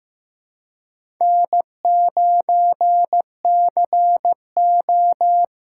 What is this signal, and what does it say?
Morse code, heard as a keyed tone, about 15 wpm, 700 Hz. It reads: N9CO
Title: USA Call Signs - 4 Characters - 3x Character Spacing - 15wpm